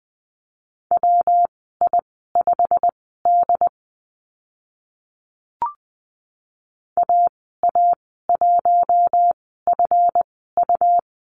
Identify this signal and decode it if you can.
Morse code, heard as a keyed tone, about 20 wpm, spaced as for 14 wpm, 700 Hz. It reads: WI5D AA1FU
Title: 179 Lesson 29g - Slash - Call Signs 20wpm